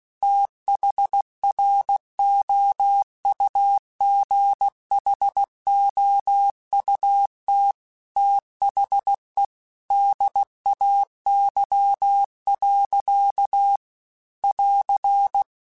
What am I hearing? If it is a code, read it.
Morse code, heard as a keyed tone, about 16 wpm, 800 Hz. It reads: THROUGHOUT THE DAY. <AR>
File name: Weather Sets / Weather Sets-mp3s / Weather 5 12 @15